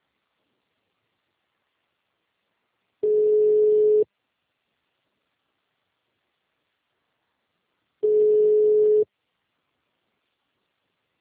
Ringback tone (Europe, 425 Hz).